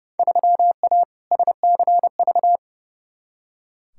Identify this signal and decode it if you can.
Morse code, heard as a keyed tone, about 30 wpm, 700 Hz. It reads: 3A SCV
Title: Field Day Exchanges - 30wpm